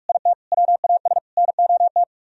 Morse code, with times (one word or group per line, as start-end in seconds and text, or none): 0.1-0.3 IT
0.5-1.2 WAS
1.4-2.0 NOT